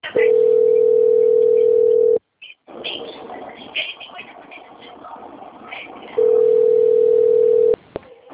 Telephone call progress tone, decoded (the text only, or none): ringback tone